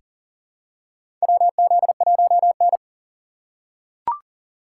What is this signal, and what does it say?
Morse code, heard as a keyed tone, about 40 wpm, 700 Hz. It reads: WZ1N
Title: USA Call Signs - 2x Character Spacing - 40wpm